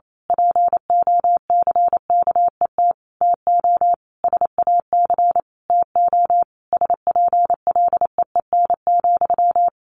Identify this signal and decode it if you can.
Morse code, heard as a keyed tone, about 28 wpm, 700 Hz. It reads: POCKET TO SAC TO SPLEEN,
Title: A Sound Of Thunder 28wpm